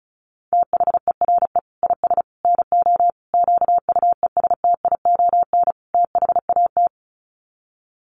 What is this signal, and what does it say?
Morse code, heard as a keyed tone, about 34 wpm, 700 Hz. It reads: THERE IS NO QUESTION THAT
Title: Sets of 5 Words Encore 35wpm